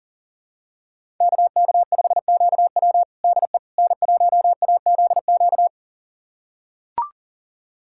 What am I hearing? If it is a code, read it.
Morse code, heard as a keyed tone, about 40 wpm, 700 Hz. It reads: KK5QW DE N1AZQ
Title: USA Callsign DE Callsign - 5 Characters - 40wpm